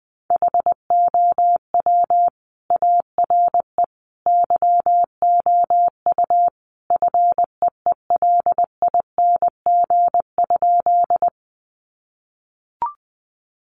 Morse code, heard as a keyed tone, about 19 wpm, 700 Hz. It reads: HOW ARE YOU FEELING?
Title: Sentences from Top 1200 Words 20wpm